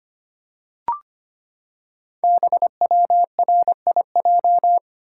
Morse code, heard as a keyed tone, about 25 wpm, 690 Hz. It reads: BWRIJ
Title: Code Groups - 5 Letters - 25wpm